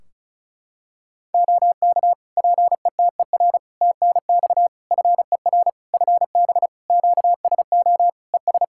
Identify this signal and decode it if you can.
Morse code, heard as a keyed tone, about 36 wpm, 700 Hz. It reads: OK PETER TNX FER FB QSO ES
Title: Practice QSO - Exchange 4 - Ending - 35wpm